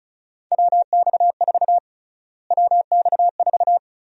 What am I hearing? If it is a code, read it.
Morse code, heard as a keyed tone, about 36 wpm, 700 Hz. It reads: WX4 WX4
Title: USA Call Sign Prefixes - Repeated 2x - 35wpm